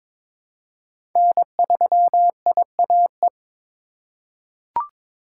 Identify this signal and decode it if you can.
Morse code, heard as a keyed tone, about 22 wpm, 700 Hz. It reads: N3IAE
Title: USA Call Signs - 5 Characters - 2x Character Spacing - 22wpm